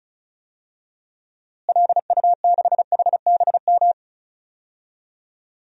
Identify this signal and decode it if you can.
Morse code, heard as a keyed tone, about 36 wpm, 700 Hz. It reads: LU6HBM
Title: DXCC-100 Call Signs - 6 Characters - Repeated 2x - 35wpm